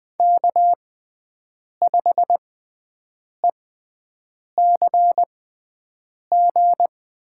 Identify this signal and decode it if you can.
Morse code, heard as a keyed tone, about 20 wpm, spaced as for 7 wpm, 700 Hz. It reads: K5ECG
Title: USA Call Signs - 5 Characters - 6x Character Spacing - 20wpm